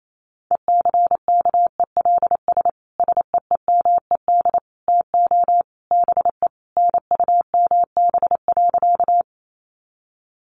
Morse code, heard as a keyed tone, about 28 wpm, 700 Hz. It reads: ECKELS SEEMED TO BE NUMB.
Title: A Sound Of Thunder 28wpm